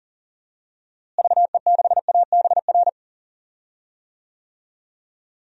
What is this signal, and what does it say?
Morse code, heard as a keyed tone, about 40 wpm, 700 Hz. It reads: VE6ABR
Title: DXCC-100 Call Signs - 6 Characters - Repeated 2x - 40wpm